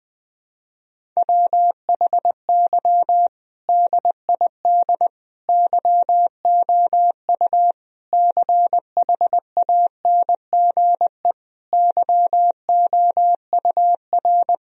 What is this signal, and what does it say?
Morse code, heard as a keyed tone, about 20 wpm, 700 Hz. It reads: WHY DID YOU CHANGE YOUR
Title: Sentences from Top 1000 Words - Review Part 1 - 20wpm